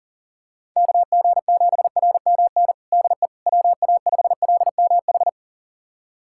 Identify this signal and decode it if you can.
Morse code, heard as a keyed tone, about 37 wpm, 700 Hz. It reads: KG7RMN DE WA5LMH
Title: USA Callsign DE Callsign - 6 Characters - 40wpm